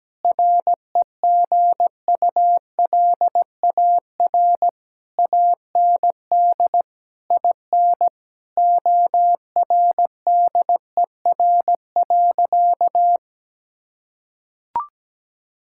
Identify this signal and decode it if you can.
Morse code, heard as a keyed tone, about 17 wpm, 700 Hz. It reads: REGULAR AND IN ORDER.
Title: Sentences from Top 1000 Words - Review Part 6 - 17wpm